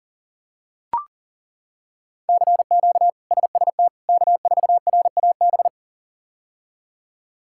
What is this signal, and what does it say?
Morse code, heard as a keyed tone, about 37 wpm, 700 Hz. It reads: CQ SST K4RAB
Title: K1USN SST QSO Practice - 40wpm